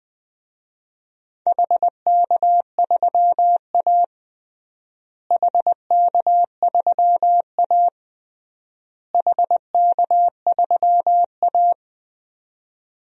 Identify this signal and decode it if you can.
Morse code, heard as a keyed tone, about 20 wpm, 700 Hz. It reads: HK3A HK3A HK3A
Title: DXCC-100 Call Signs - 4 Characters - Repeated 3x - 20wpm